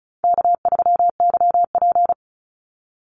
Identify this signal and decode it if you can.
Morse code, heard as a keyed tone, about 34 wpm, 700 Hz. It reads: K3YP
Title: Call Signs Repeated 2x - 35wpm